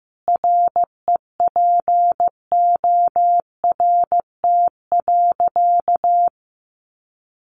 Morse code, heard as a keyed tone, about 15 wpm, 700 Hz. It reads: REPORT.
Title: Sentences from Top 1300 Words 15wpm